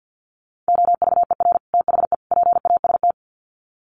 Morse code, heard as a keyed tone, about 48 wpm, 700 Hz. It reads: OVER THE PAST
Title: Sets of 3 Words 50wpm